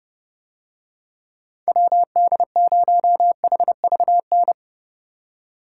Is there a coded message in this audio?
Yes: Morse code from 1.7 to 4.5 s, heard as a keyed tone, about 30 wpm, 700 Hz. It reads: WD0HVN